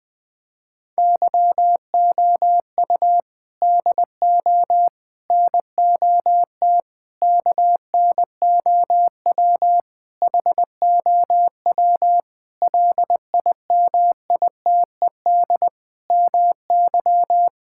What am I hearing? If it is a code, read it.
Morse code, heard as a keyed tone, about 19 wpm, 700 Hz. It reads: YOU DO NOT KNOW HOW LIMITED MY
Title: Sentences from Top 1000 Words - Review Part 6 - 20wpm